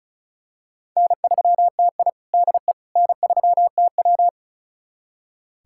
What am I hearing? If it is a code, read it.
Morse code, heard as a keyed tone, about 36 wpm, 700 Hz. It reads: N3TI DE N3TW
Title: USA Callsign DE Callsign - 4 Characters - 35wpm